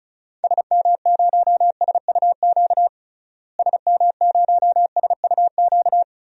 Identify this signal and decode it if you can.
Morse code, heard as a keyed tone, about 36 wpm, 700 Hz. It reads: SM0SUQ SM0SUQ
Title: DXCC-100 Call Signs - 6 Characters - Repeated 3x - 35wpm